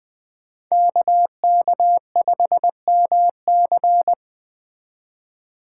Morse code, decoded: KK5MC

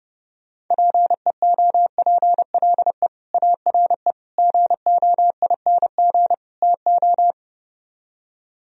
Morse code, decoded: PEOPLE ARE GOING TO

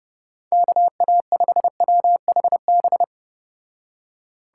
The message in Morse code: KA5WHB

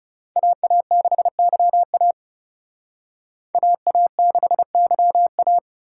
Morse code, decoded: AA6YA AA6YA

